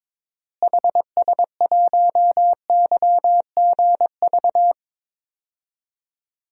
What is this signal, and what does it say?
Morse code, heard as a keyed tone, about 22 wpm, 700 Hz. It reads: HS1YGV
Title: DXCC-100 Call Signs - 6 Characters - Repeated 2x - 22wpm